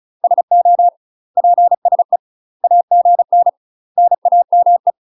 Morse code, heard as a keyed tone, about 34 wpm, 700 Hz. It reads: SO PSE AGN NAME
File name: UK MASTER QSOs TRAINING / 35WPM / UK MASTER QSOs 0043